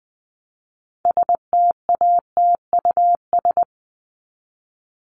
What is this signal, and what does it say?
Morse code, heard as a keyed tone, about 20 wpm, 700 Hz. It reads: STATUS
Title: Top 1251-1500 Words 20wpm